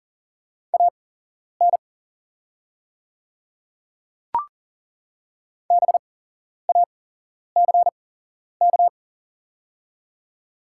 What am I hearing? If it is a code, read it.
Morse code, heard as a keyed tone, about 40 wpm, spaced as for 11 wpm, 700 Hz. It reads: AN BACK
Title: 8x Character Spacing - Top 200 Words - 40wpm